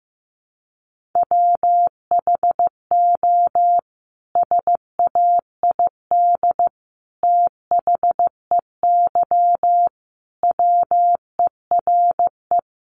Morse code, heard as a keyed tone, about 15 wpm, 700 Hz. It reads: WHO SAID THEY WERE
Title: Sentences from Top 700 Words 15wpm